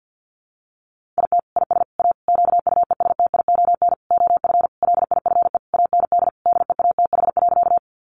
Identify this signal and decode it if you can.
Morse code, heard as a keyed tone, about 48 wpm, 700 Hz. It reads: IT IS A QUESTION OF LIFE AND DEATH.
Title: Sentences from Top 1100 Words 50wpm